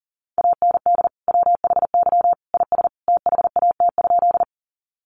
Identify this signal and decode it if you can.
Morse code, heard as a keyed tone, about 40 wpm, 700 Hz. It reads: AND WHY IS THAT?
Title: Sentences from Top 200 Words 40wpm